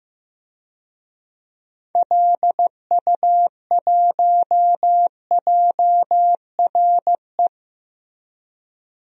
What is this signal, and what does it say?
Morse code, heard as a keyed tone, about 15 wpm, 700 Hz. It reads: LU1JRE
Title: DXCC-100 Call Signs - 6 Characters - Repeated 3x - 15wpm